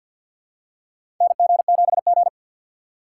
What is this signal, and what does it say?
Morse code, heard as a keyed tone, about 47 wpm, 690 Hz. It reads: NG7G
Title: USA Call Signs - 2x Character Spacing - 50wpm